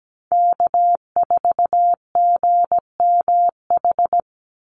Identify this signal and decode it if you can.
Morse code, heard as a keyed tone, about 17 wpm, 700 Hz. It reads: K4GMH
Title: Call Signs Repeated 2x - 17wpm